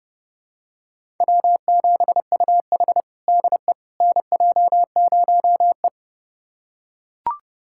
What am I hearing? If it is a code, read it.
Morse code, heard as a keyed tone, about 30 wpm, 700 Hz. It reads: W7UH DE NJ0E
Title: USA Callsign DE Callsign - 4 Characters - 30wpm